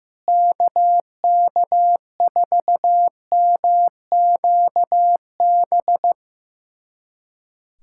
Morse code, heard as a keyed tone, about 15 wpm, 700 Hz. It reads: KK4MQB
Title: USA Callsign DE Callsign - 6 Characters - 15wpm